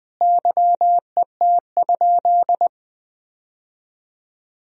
Morse code, heard as a keyed tone, about 19 wpm, 700 Hz. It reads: YET?
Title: Sentences from Top 1000 Words - Review Part 2 - 20wpm